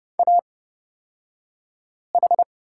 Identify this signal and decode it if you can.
Morse code, heard as a keyed tone, about 31 wpm, spaced as for 5 wpm, 700 Hz. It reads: AH
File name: Lesson 36d - BT - Two Characters 30wpm